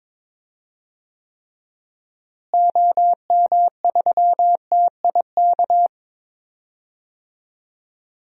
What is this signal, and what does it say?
Morse code, heard as a keyed tone, about 21 wpm, 700 Hz. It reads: OM3TIK
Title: DXCC-100 Call Signs - 6 Characters - Repeated 3x - 22wpm